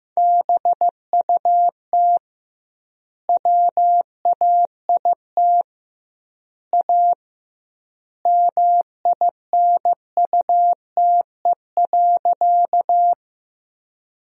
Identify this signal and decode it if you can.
Morse code, heard as a keyed tone, about 15 wpm, 700 Hz. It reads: BUT WAIT A MINUTE.
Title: 2x Word-Spacing - Sentences from Top 500 Words - 15wpm